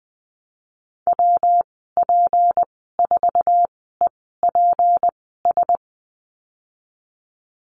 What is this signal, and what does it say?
Morse code, heard as a keyed tone, about 20 wpm, spaced as for 15 wpm, 700 Hz. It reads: WP4EPS
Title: 179 Lesson 29g - Slash - Call Signs 20wpm